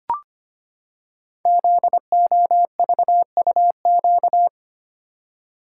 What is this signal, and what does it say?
Morse code, heard as a keyed tone, about 25 wpm, 700 Hz. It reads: ZOVUQ
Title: Code Groups - 5 Letters - 25wpm